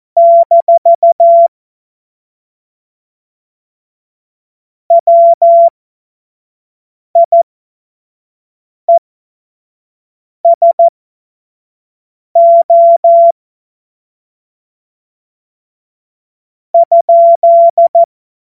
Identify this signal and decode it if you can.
Morse code, heard as a keyed tone, about 14 wpm, spaced as for 5 wpm, 680 Hz. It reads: - WIESO ?
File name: Morsekurs / Witze / Witz 02 April 2024 25 F70